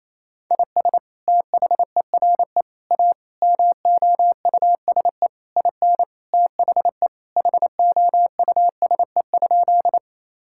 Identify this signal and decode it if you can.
Morse code, heard as a keyed tone, about 28 wpm, 700 Hz. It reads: IS THERE A MOUSE IN THE HOUSE?
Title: Sentences from Top 1000 Words - Review Part 5 - 28wpm